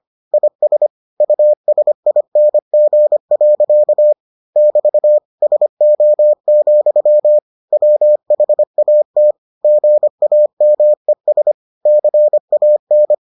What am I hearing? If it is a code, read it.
Morse code, heard as a keyed tone, about 25 wpm, 600 Hz. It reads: IS USING. <BT> SO, WHAT GAMES CAN